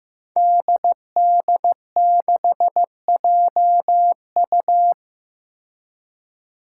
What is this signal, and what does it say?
Morse code, heard as a keyed tone, about 16 wpm, 700 Hz. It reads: DD6JU
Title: Five Challenging Letter-Number Combinations 15wpm